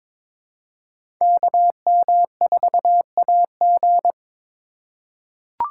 Morse code, heard as a keyed tone, about 22 wpm, 700 Hz. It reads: KM4AG
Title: USA Call Signs - 2x Character Spacing - 22wpm